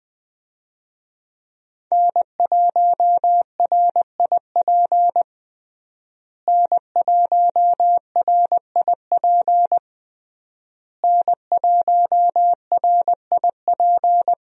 Morse code, heard as a keyed tone, about 20 wpm, 700 Hz. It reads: N1RIP N1RIP N1RIP
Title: USA Call Signs - 5 Characters - Repeated 3x - 20wpm